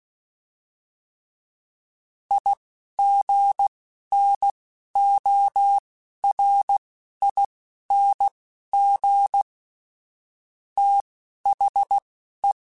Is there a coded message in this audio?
Yes: Morse code from 2.3 to 12.5 s, heard as a keyed tone, about 16 wpm, spaced as for 12 wpm, 800 Hz. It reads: IGNORING THE